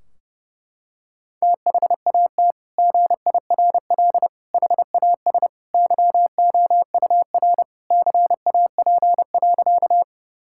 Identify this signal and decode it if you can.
Morse code, heard as a keyed tone, about 30 wpm, 700 Hz. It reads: THAT GIRL HAS YOUR CAP.
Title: Sentences from Top 1600 Words 30wpm